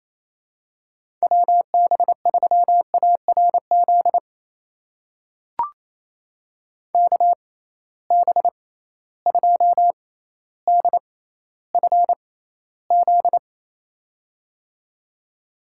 Morse code, heard as a keyed tone, about 28 wpm, spaced as for 14 wpm, 700 Hz. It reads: WB3ARZ KB2DFZ
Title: USA Call Signs - 6 Characters - 6x Character Spacing - 28wpm